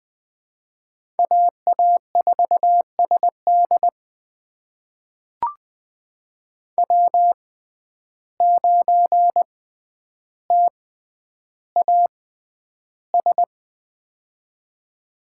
Morse code, heard as a keyed tone, about 21 wpm, spaced as for 10 wpm, 700 Hz. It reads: AA4SD W9TAS
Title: USA Call Signs - 5 Characters - 6x Character Spacing - 20wpm